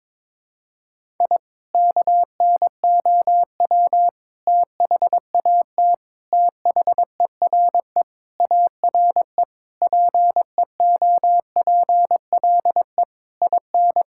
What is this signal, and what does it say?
Morse code, heard as a keyed tone, about 22 wpm, 700 Hz. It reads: I KNOW THAT THERE ARE PEOPLE IN
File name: practice / 22wpm / Sentences with 3-Word Phrases - 22wpm